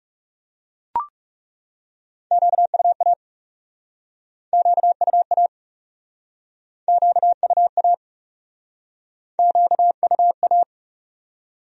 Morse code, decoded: QUA QUA QUA QUA

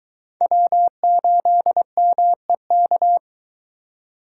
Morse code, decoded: W8MEK

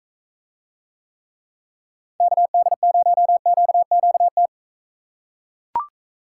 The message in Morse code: KD0QQT

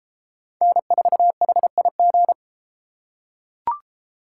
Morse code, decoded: N4HIG

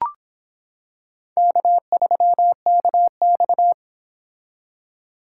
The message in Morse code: K3KX

